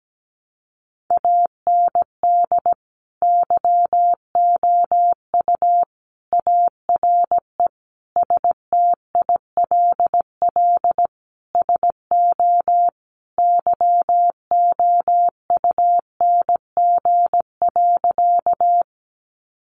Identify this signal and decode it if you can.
Morse code, heard as a keyed tone, about 17 wpm, 700 Hz. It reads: AND YOU ARE STILL SO YOUNG.
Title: Sentences from Top 400 Words 17wpm